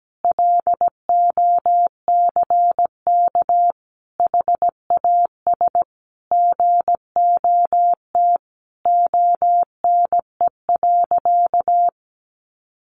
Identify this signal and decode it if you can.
Morse code, heard as a keyed tone, about 17 wpm, 700 Hz. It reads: LOCK HAS GOT ONE.
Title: Sentences from Top 1300 Words 17wpm